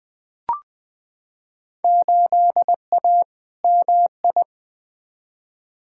Morse code, heard as a keyed tone, about 20 wpm, 700 Hz. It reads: 8A MI
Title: Field Day Exchanges Encore - 20wpm